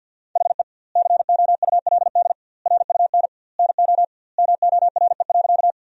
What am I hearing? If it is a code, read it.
Morse code, heard as a keyed tone, about 48 wpm, 700 Hz. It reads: HE COULD RUN NO MORE.
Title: 2x Word-Spacing - Sentences from Top 300 Words - 50wpm